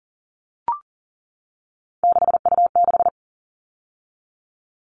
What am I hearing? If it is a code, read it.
Morse code, heard as a keyed tone, about 40 wpm, 700 Hz. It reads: 6U6
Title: All Cut Numbers - 3-Digit 40wpm